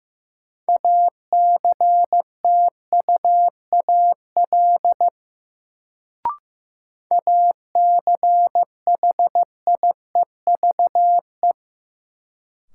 Morse code, heard as a keyed tone, about 15 wpm, 700 Hz. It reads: ACTUAL ACHIEVE
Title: Sentences from Top 1000 Words - Review Part 5 - New Word Drill - 15wpm